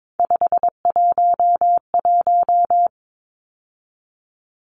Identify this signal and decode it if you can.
Morse code, heard as a keyed tone, about 22 wpm, 700 Hz. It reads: 511
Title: Numbers - 3-Digit 22wpm